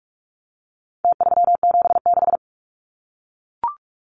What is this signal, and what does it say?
Morse code, heard as a keyed tone, about 44 wpm, 700 Hz. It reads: T376
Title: Numbers - Contest Style - 4-Digit 45wpm